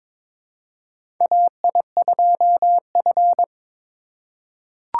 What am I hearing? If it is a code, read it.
Morse code, heard as a keyed tone, about 21 wpm, 700 Hz. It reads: AI2F
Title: USA Call Signs - 4 Characters - 2x Character Spacing - 22wpm